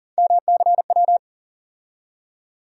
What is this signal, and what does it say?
Morse code, heard as a keyed tone, about 38 wpm, 700 Hz. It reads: MCW
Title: USA Call Sign Postfixes - 3 Letters 40wpm